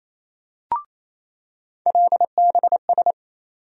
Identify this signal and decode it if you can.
Morse code, heard as a keyed tone, about 28 wpm, 700 Hz. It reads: LBS
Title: USA Call Sign Postfixes - 3 Letters 28wpm